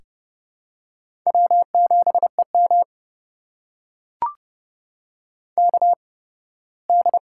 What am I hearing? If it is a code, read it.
Morse code, heard as a keyed tone, about 30 wpm, spaced as for 18 wpm, 700 Hz. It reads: W7EM KD
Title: USA Call Signs - 4 Characters - 8x Character Spacing - 30wpm